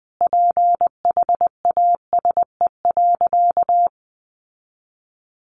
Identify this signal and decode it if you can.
Morse code, heard as a keyed tone, about 20 wpm, 700 Hz. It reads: PHASE.
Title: Sentences from Top 1200 Words 20wpm